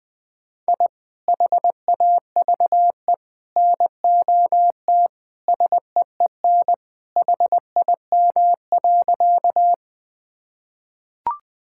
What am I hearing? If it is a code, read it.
Morse code, heard as a keyed tone, about 20 wpm, 700 Hz. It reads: I HAVE NOT SEEN HIM.
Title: Sentences with 3-Word Phrases - Spoken Beginning - 20wpm